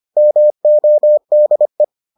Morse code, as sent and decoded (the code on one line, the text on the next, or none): -- --- -.. .
MODE